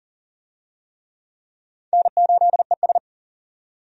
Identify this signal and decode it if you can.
Morse code, heard as a keyed tone, about 40 wpm, 700 Hz. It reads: N8ES
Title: USA Call Signs - 4 Characters - 40wpm